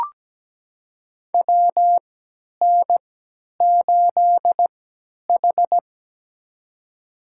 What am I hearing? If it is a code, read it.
Morse code, heard as a keyed tone, about 17 wpm, spaced as for 10 wpm, 700 Hz. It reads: WN8H
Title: USA Call Signs - 3x Character Spacing - 17wpm